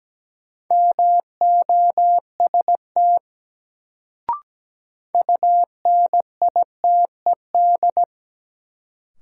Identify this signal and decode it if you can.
Morse code, heard as a keyed tone, about 17 wpm, 700 Hz. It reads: MOST UNITED
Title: Sets of 2 Words - New Word Drill - 17wpm